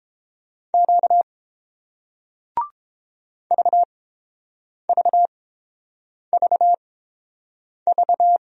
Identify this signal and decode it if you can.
Morse code, heard as a keyed tone, about 30 wpm, spaced as for 7 wpm, 700 Hz. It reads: Q VVVV